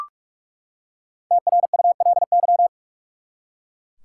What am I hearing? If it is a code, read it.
Morse code, heard as a keyed tone, about 43 wpm, 700 Hz. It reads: TRULY
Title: Five Letter Words - 45wpm